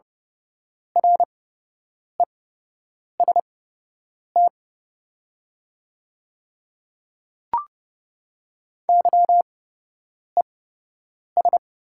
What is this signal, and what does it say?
Morse code, heard as a keyed tone, about 30 wpm, spaced as for 8 wpm, 700 Hz. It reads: REST YES
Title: 8x Character Spacing - Top 500 Words - 30wpm